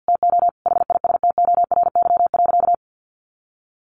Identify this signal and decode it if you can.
Morse code, heard as a keyed tone, about 50 wpm, 700 Hz. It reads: TO HISTORY.
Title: A Sound Of Thunder 50wpm